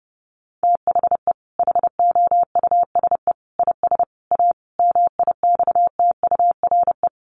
Morse code, heard as a keyed tone, about 30 wpm, 700 Hz. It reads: THE HOUSE IS A MIXTURE
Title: Sentences from Top 1500 Words 30wpm